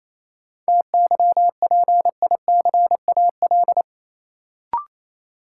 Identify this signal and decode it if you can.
Morse code, heard as a keyed tone, about 28 wpm, 700 Hz. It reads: TYPICAL
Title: Sentences from Top 1100 Words - New Word Drill Alt - 28wpm